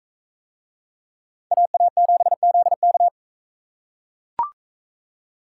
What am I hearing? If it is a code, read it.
Morse code, heard as a keyed tone, about 41 wpm, 700 Hz. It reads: AA7ZK